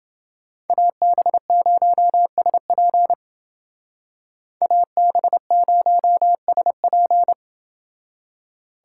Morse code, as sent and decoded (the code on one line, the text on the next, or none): .- -... ----- ... .--. / .- -... ----- ... .--.
AB0SP AB0SP